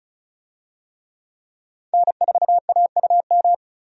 Morse code, heard as a keyed tone, about 36 wpm, 700 Hz. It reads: N4AUM
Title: USA Call Signs - 5 Characters - 35wpm